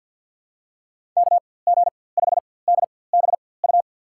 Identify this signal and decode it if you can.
Morse code, heard as a keyed tone, about 47 wpm, spaced as for 23 wpm, 700 Hz. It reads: KG5DBU
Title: USA Call Signs - 6 Characters - 4x Character Spacing - 50wpm